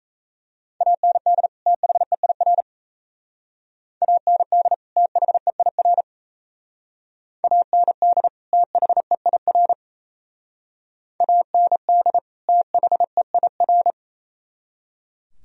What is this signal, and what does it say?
Morse code, heard as a keyed tone, about 34 wpm, 700 Hz. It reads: AND THEIR AND THEIR AND THEIR AND THEIR
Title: Speed-Racing - Sets of 2 Words 28wpm